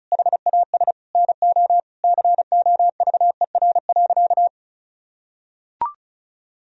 Morse code, heard as a keyed tone, about 36 wpm, 700 Hz. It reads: HAS NO COVER.
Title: Sentences from Top 1000 Words - Review Part 1 - 35wpm